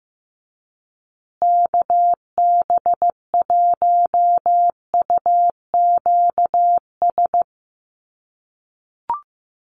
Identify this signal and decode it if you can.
Morse code, heard as a keyed tone, about 15 wpm, 700 Hz. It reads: KB1UQS
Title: USA Call Signs Repeated 3x 15wpm